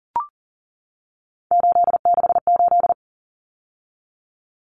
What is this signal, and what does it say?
Morse code, heard as a keyed tone, about 40 wpm, 700 Hz. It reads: 868
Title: Numbers - 3-Digit 40wpm